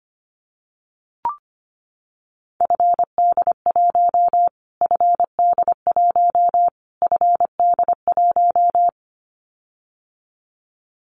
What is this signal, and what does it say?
Morse code, heard as a keyed tone, about 25 wpm, 700 Hz. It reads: FD1 FD1 FD1
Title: Callsign Prefixes for Top 25 DXCC - Repeated 3x - 25wpm